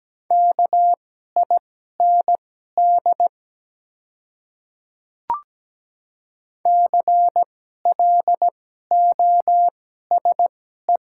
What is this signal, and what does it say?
Morse code, heard as a keyed tone, about 17 wpm, spaced as for 12 wpm, 690 Hz. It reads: KIND CLOSE
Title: 2x Character Spacing - Top 300 Words - 17wpm